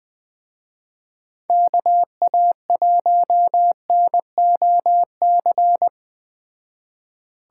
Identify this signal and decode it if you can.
Morse code, heard as a keyed tone, about 20 wpm, 700 Hz. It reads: KA1NOC
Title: Speed Racing - USA 2x3 Call Signs 17wpm